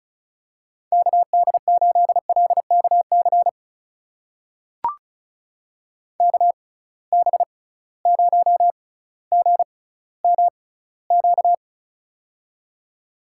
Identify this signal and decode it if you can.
Morse code, heard as a keyed tone, about 36 wpm, spaced as for 18 wpm, 700 Hz. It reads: KD8LKC KB0GMQ
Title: USA Call Signs - 6 Characters - 6x Character Spacing - 35wpm